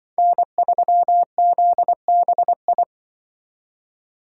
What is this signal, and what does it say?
Morse code, heard as a keyed tone, about 23 wpm, 700 Hz. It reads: N3ZBI
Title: Speed Racing - USA 1x3 Call Signs 20wpm